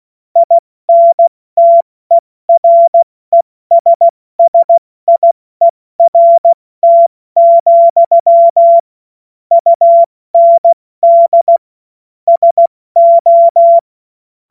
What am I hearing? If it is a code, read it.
Morse code, heard as a keyed tone, about 16 wpm, 680 Hz. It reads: INTERESSIERT, UND SO
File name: Morsekurs / Witze / Witz 09 Januar 2024 70 F80